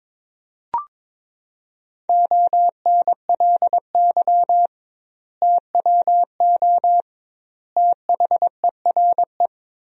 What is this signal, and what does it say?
Morse code, heard as a keyed tone, about 22 wpm, 700 Hz. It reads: ONLY TWO THERE
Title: 2x Word-Spacing - Sentences from Top 200 Words - 22wpm